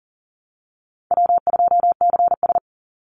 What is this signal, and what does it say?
Morse code, heard as a keyed tone, about 40 wpm, 700 Hz. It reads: W2CS
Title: Call Signs Repeated 3x - 40wpm